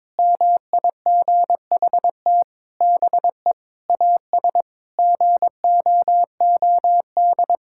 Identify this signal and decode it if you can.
Morse code, heard as a keyed tone, about 22 wpm, 700 Hz. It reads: MIGHT BE AS GOOD
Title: Sentences from Top 1600 Words 22wpm